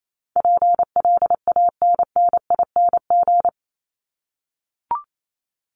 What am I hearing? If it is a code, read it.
Morse code, heard as a keyed tone, about 28 wpm, 700 Hz. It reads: PLANNING
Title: ING Suffix - Top 100 Words with Base Words 28wpm